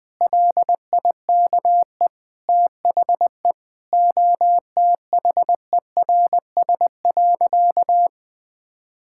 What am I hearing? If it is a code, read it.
Morse code, heard as a keyed tone, about 20 wpm, 700 Hz. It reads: LIKE THE OTHERS.
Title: Sentences with 8-Word Phrases - 20wpm